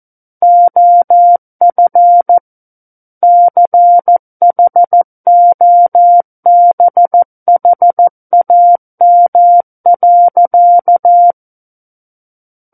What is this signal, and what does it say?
Morse code, heard as a keyed tone, about 14 wpm, 700 Hz. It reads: OF CHOBHAM.
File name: Book One / 05-Chapter 5 - 14 WPM